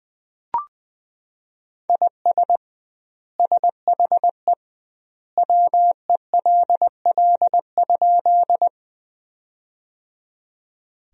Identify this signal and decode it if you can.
Morse code, heard as a keyed tone, about 21 wpm, 700 Hz. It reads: IS SHE WELL?
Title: 2x Word-Spacing - Sentences from Top 200 Words - 20wpm